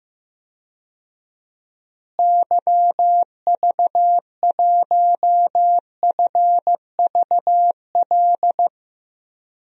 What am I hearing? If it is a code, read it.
Morse code, heard as a keyed tone, about 15 wpm, 700 Hz. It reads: YV1FVL
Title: DXCC-100 Call Signs - 6 Characters - Repeated 2x - 15wpm